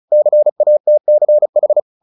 Morse code, heard as a keyed tone, about 34 wpm, 600 Hz. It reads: CATCH
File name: the old man and the sea / the old man and the sea 1 35wpm